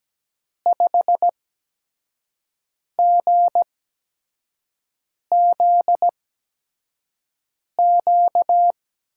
Morse code, heard as a keyed tone, about 17 wpm, spaced as for 5 wpm, 700 Hz. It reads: 5GZQ
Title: USA Call Signs - 6 Characters - 8x Character Spacing - 17wpm